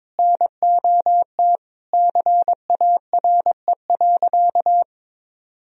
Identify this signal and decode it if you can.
Morse code, heard as a keyed tone, about 22 wpm, 700 Hz. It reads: NOT CARE.
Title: Sentences from Top 1000 Words - Review Part 1 - 22wpm